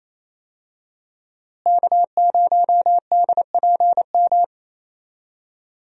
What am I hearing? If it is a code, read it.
Morse code, heard as a keyed tone, about 28 wpm, 700 Hz. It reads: K0DPM
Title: USA Call Signs - 5 Characters - Repeated 2x - 28wpm